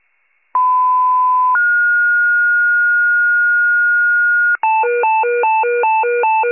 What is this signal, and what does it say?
Two-tone page, heard as a one-second tone, then a three-second tone, about 1010 then 1430 Hz. It is heard from 0.5 to 4.6 s.
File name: wcalert